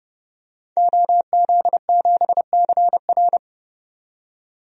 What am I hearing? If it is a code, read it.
Morse code, heard as a keyed tone, about 30 wpm, 700 Hz. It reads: OZ7CR
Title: DXCC-100 Call Signs - 5 Characters - 30wpm